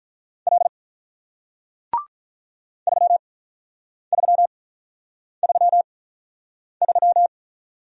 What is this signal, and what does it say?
Morse code, heard as a keyed tone, about 48 wpm, spaced as for 9 wpm, 700 Hz. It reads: L 3333